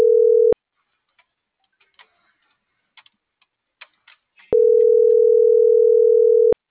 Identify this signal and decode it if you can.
Ringback tone (North America), 440 + 480 Hz.